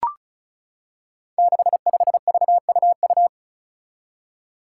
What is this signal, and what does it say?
Morse code, heard as a keyed tone, about 34 wpm, 700 Hz. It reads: 65VUU
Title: Five Challenging Letter-Number Combinations 35wpm